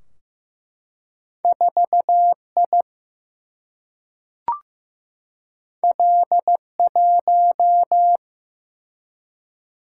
Morse code, heard as a keyed tone, about 15 wpm, 700 Hz. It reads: 4I L1
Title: Single-Letter Single-Number 15wpm